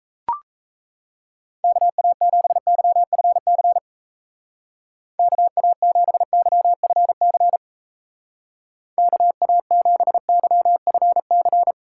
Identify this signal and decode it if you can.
Morse code, heard as a keyed tone, about 34 wpm, 700 Hz. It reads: KA7YFC KA7YFC KA7YFC
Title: Speed Racing - USA 2x3 Call Signs 28wpm